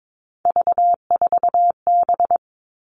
Morse code, heard as a keyed tone, about 22 wpm, 700 Hz. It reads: V4B
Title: All Cut Numbers - 3-Digit 22wpm